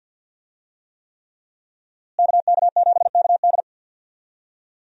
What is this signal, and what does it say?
Morse code, heard as a keyed tone, about 52 wpm, 700 Hz. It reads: KK7KD
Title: USA Call Signs - 5 Characters - Repeated 3x - 50wpm